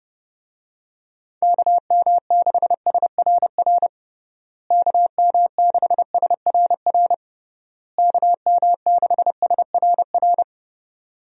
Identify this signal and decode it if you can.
Morse code, heard as a keyed tone, about 30 wpm, 700 Hz. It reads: KM6SRR KM6SRR KM6SRR
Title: USA Call Signs - 6 Characters - Repeated 3x - 30wpm